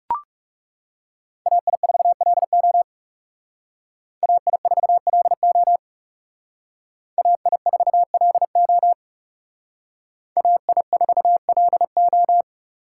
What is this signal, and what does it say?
Morse code, heard as a keyed tone, about 36 wpm, 700 Hz. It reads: AI4LO AI4LO AI4LO AI4LO
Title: Speed Racing - USA Call Signs 30wpm